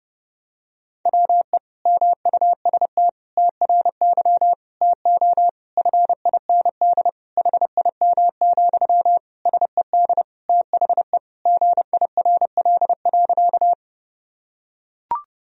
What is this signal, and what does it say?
Morse code, heard as a keyed tone, about 30 wpm, 700 Hz. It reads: WE MUST TRY TO FIND HIM, SED THE GIRL.
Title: Sentences from Top 400 Words and Abbreviations 30wpm